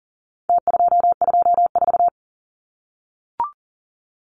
Morse code, heard as a keyed tone, about 40 wpm, 700 Hz. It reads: T224
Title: Numbers - Contest Style - 4-Digit 40wpm